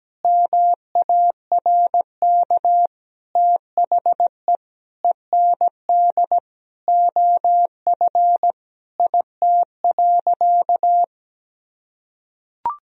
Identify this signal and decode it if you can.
Morse code, heard as a keyed tone, about 17 wpm, 700 Hz. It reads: MARK THE END OF IT.
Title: Sentences from Top 1000 Words - Review Part 1 - 17wpm